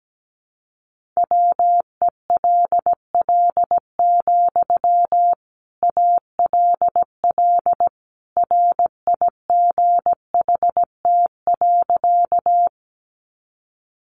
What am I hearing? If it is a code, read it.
Morse code, heard as a keyed tone, about 17 wpm, 700 Hz. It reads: WELL, ALL RIGHT.
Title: Sentences from Top 200 Words 17wpm